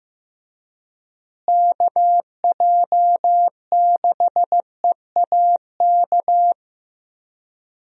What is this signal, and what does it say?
Morse code, heard as a keyed tone, about 16 wpm, 700 Hz. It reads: KJ6EAK
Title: USA Call Signs - 6 Characters - Repeated 2x - 15wpm